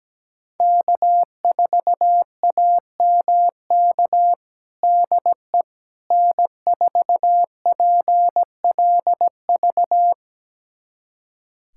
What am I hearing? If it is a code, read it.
Morse code, heard as a keyed tone, about 17 wpm, 700 Hz. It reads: K4AMK DE N4PLV
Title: USA Callsign DE Callsign - 5 Characters - 17wpm